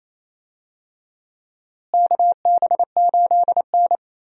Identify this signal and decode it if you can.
Morse code, heard as a keyed tone, about 27 wpm, 700 Hz. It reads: KB8N
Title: USA Call Signs - 4 Characters - Repeated 3x - 28wpm